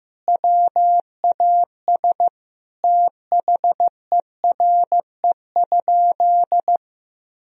Morse code, heard as a keyed tone, about 15 wpm, 700 Hz. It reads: WAS THERE?
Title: Sentences from Top 1000 Words - Review Part 4 - 15wpm